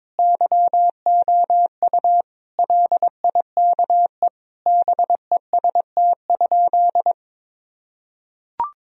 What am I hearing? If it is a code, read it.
Morse code, heard as a keyed tone, about 22 wpm, 700 Hz. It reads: YOU LIKE BEST?
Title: Sentences from Top 1000 Words - Review Part 1 - 22wpm